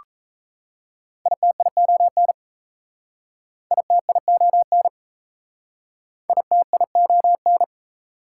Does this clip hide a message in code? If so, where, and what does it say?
Yes: Morse code from 1.3 to 7.6 s, heard as a keyed tone, about 36 wpm, 700 Hz. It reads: ITION ITION ITION